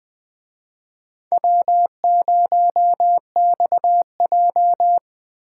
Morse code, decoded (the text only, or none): W0XJ